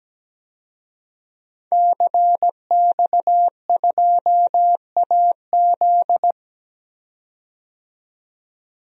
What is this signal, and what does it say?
Morse code, heard as a keyed tone, about 17 wpm, 700 Hz. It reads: CX2AZ
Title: DXCC-100 Call Signs - 5 Characters - 17wpm